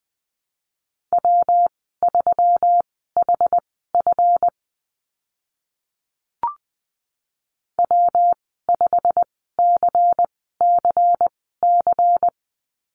Morse code, heard as a keyed tone, about 20 wpm, spaced as for 14 wpm, 700 Hz. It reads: W3HF W5CCC
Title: 140 Lesson 23g - Letter f - Call Signs 20wpm